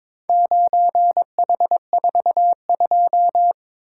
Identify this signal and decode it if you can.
Morse code, heard as a keyed tone, about 22 wpm, 700 Hz. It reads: 9H42
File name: practice / 22wpm / Four Letter-Number Combinations - 22wpm